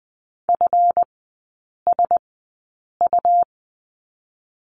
Morse code, heard as a keyed tone, about 20 wpm, spaced as for 8 wpm, 700 Hz. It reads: FSU
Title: 138 Lesson 23e - Letter f - Three Characters 20wpm